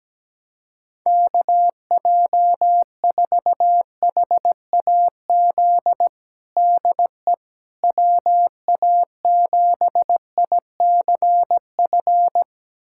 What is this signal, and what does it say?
Morse code, heard as a keyed tone, about 17 wpm, 700 Hz. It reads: KJ4HAZ DE WA7ICF
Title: USA Callsign DE Callsign - 6 Characters - 17wpm